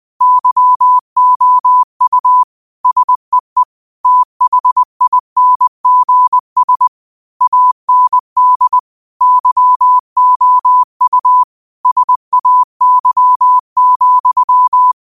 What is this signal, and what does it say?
Morse code, heard as a keyed tone, about 20 wpm, 1000 Hz. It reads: YOU SEE THINGS AND YOU SAY,
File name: qotdfiles / 20wpm